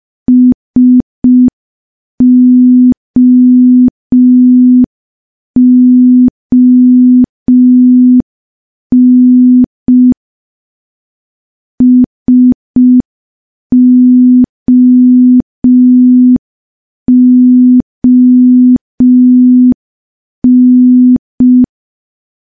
Morse code, heard as a keyed tone, about 5 wpm, 250 Hz. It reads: SOON SOON
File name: morse / morse-training-05WPM